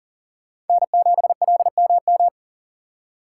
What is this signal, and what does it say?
Morse code, heard as a keyed tone, about 51 wpm, 700 Hz. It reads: N7LMM